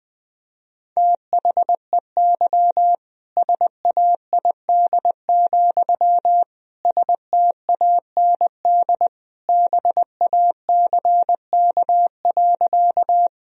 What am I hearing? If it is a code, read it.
Morse code, heard as a keyed tone, about 20 wpm, 700 Hz. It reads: THEY SAID, STAND BACK.
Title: Sentences from Top 1000 Words - Review Part 1 - 20wpm